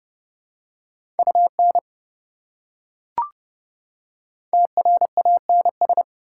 Morse code, heard as a keyed tone, about 29 wpm, 700 Hz. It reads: UN TRANS